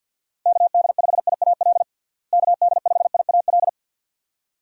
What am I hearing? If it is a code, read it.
Morse code, heard as a keyed tone, about 53 wpm, 700 Hz. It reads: KD5IAL KD5IAL
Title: USA Call Signs - 6 Characters - Repeated 2x - 50wpm